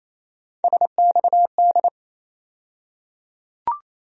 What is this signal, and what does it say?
Morse code, heard as a keyed tone, about 28 wpm, 700 Hz. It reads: SXD
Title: USA Call Sign Postfixes - 3 Letters 28wpm